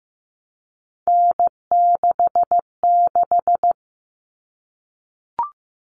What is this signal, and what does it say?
Morse code, heard as a keyed tone, about 15 wpm, 700 Hz. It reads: N66
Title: All Cut Numbers - 3-Digit 15wpm